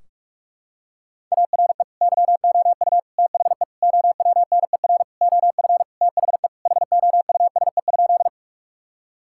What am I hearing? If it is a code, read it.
Morse code, heard as a keyed tone, about 47 wpm, 700 Hz. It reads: ARE YOU THE OWNER OF THE HOUSE?
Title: Sentences from Top 1600 Words 45wpm